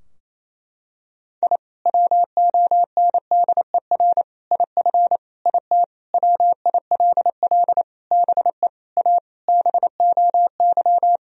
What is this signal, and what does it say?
Morse code, heard as a keyed tone, about 28 wpm, 690 Hz. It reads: I WONDER IF IT WILL BE A BOY
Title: Sentences from Top 1000 Words - Review Part 3 - 28wpm